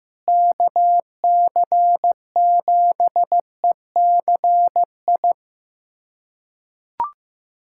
Morse code, heard as a keyed tone, about 15 wpm, 700 Hz. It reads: KC7ECI
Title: USA Callsign DE Callsign - 6 Characters - 15wpm